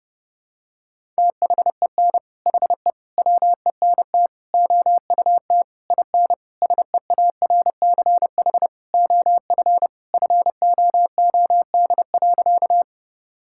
Morse code, heard as a keyed tone, about 32 wpm, 700 Hz. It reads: THEN HE WENT OUT IN SEARCH OF FOOD.
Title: Sentences from Top 1000 Words - Review Part 5 - 30wpm